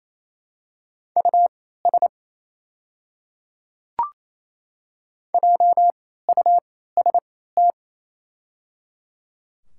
Morse code, heard as a keyed tone, about 28 wpm, spaced as for 16 wpm, 700 Hz. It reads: US JUST
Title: 3x Character Spacing - Top 200 Words - 28wpm